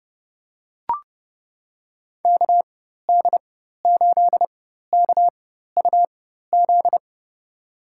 Morse code, heard as a keyed tone, about 30 wpm, spaced as for 14 wpm, 700 Hz. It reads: KD8KUZ
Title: USA Call Signs - 6 Characters - 4x Character Spacing - 30wpm